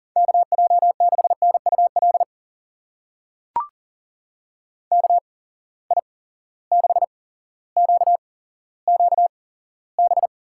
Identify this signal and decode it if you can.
Morse code, heard as a keyed tone, about 40 wpm, spaced as for 17 wpm, 700 Hz. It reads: KJ6NUL KI6QQB
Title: USA Call Signs - 6 Characters - 8x Character Spacing - 40wpm